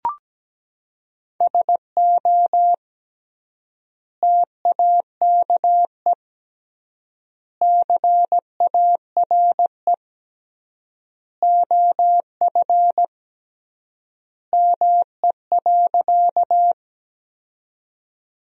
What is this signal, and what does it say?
Morse code, heard as a keyed tone, about 17 wpm, 700 Hz. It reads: SO TAKE CARE OF ME.
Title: 3x Word-Spacing - Sentences from Top 300 Words - 17wpm